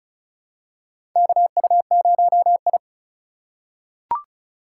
Morse code, heard as a keyed tone, about 34 wpm, 700 Hz. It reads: KU0I
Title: USA Call Signs - 4 Characters - 3x Character Spacing - 35wpm